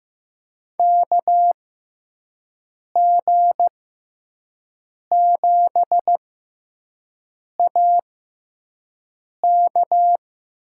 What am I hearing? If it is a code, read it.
Morse code, heard as a keyed tone, about 15 wpm, spaced as for 5 wpm, 700 Hz. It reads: KG7AK
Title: USA Call Signs - 6 Characters - 6x Character Spacing - 15wpm